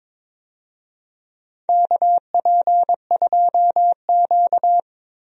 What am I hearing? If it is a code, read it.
Morse code, heard as a keyed tone, about 22 wpm, 700 Hz. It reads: KP2Q
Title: DXCC-100 Call Signs - 4 Characters - Repeated 2x - 22wpm